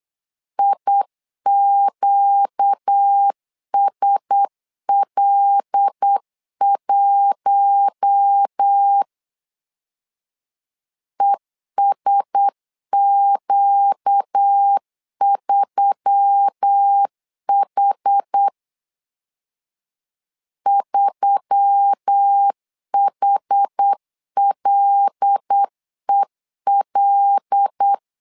Morse code, heard as a keyed tone, about 8 wpm, 800 Hz. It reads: IQSL1 ESQ3H 3HLEL
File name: LEZIONI / 5 LBVC / 5 L 8wpm 2831